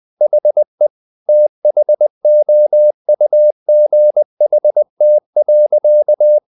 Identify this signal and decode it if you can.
Morse code, heard as a keyed tone, about 20 wpm, 600 Hz. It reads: HE THOUGHT.
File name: the old man and the sea / the old man and the sea part 1 20wpm